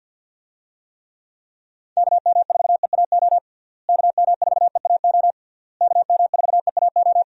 Morse code, heard as a keyed tone, about 51 wpm, 700 Hz. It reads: KM4EAO KM4EAO KM4EAO
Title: USA Call Signs - 6 Characters - Repeated 3x - 50wpm